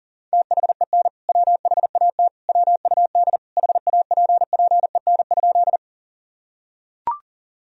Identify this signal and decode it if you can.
Morse code, heard as a keyed tone, about 37 wpm, 700 Hz. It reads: THEN WHAT WUD HAPPEN?
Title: Sentences from Top 400 Words and Abbreviations 40wpm